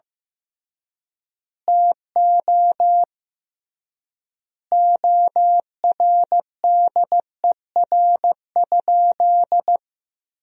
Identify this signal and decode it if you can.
Morse code, heard as a keyed tone, about 15 wpm, 700 Hz. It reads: TO ORDER?
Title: 3x Word-Spacing - Sentences from Top 400 Words - 15wpm